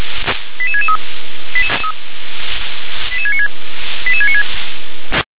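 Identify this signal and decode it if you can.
Five-tone selective call (ZVEI), heard as quick runs of five tones, tones of about 80 ms. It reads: 88663 88696 88696
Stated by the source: ZVEI five-tone call received by SDR.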